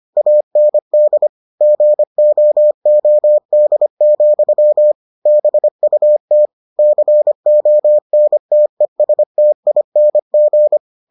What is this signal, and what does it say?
Morse code, heard as a keyed tone, about 25 wpm, 600 Hz. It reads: AND GOOD, BUT CONTESTING